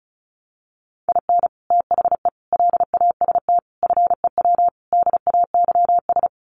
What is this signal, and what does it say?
Morse code, heard as a keyed tone, about 34 wpm, 700 Hz. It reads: IN THE LAST FEW DAYS
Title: Sets of 5 Words Encore2 35wpm